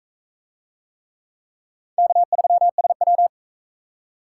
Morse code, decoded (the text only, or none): K3SW